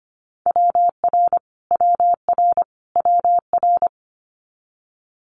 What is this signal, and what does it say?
Morse code, heard as a keyed tone, about 25 wpm, 700 Hz. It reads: WR WR WR
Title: Multi-letter Phonemes Repeated 3x 25wpm